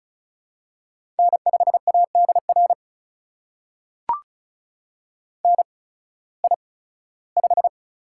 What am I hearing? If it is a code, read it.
Morse code, heard as a keyed tone, about 36 wpm, spaced as for 19 wpm, 700 Hz. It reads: N5ADR NI5
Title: USA Call Signs - 8x Character Spacing - 35wpm